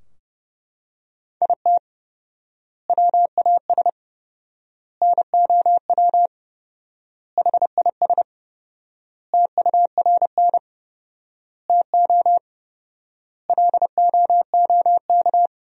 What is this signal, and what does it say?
Morse code, heard as a keyed tone, about 30 wpm, 700 Hz. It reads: IT WAS NOW HIS TURN TO LOOK
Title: 4x Word-Spacing - Sentences from Top 200 Words - 30wpm